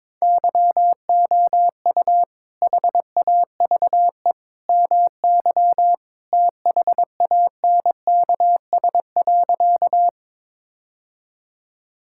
Morse code, heard as a keyed tone, about 22 wpm, 700 Hz. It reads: YOU HAVE MY THANKS.
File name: practice / 22wpm / Sentences with 2-Word Phrases - 22wpm